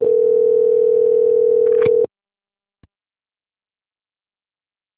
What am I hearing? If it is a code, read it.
Ringback tone (North America), 440 + 480 Hz.
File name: out- - - - - .16640